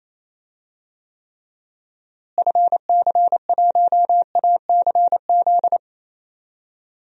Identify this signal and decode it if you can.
Morse code, heard as a keyed tone, about 27 wpm, 700 Hz. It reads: FC1ACZ
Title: DXCC-100 Call Signs - 6 Characters - Repeated 2x - 28wpm